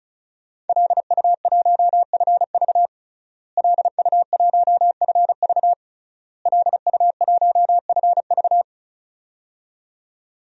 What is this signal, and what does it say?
Morse code, heard as a keyed tone, about 36 wpm, 700 Hz. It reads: LU1FV LU1FV LU1FV
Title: DXCC-100 Call Signs - 5 Characters - Repeated 3x - 35wpm